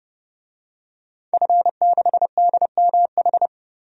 Morse code, heard as a keyed tone, about 30 wpm, 700 Hz. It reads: F6DMH
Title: DXCC-100 Call Signs - 5 Characters - Repeated 3x - 30wpm